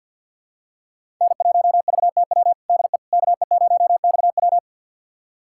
Morse code, heard as a keyed tone, about 53 wpm, 700 Hz. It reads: N1VTW DE KE0XW
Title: USA Callsign DE Callsign - 5 Characters - 50wpm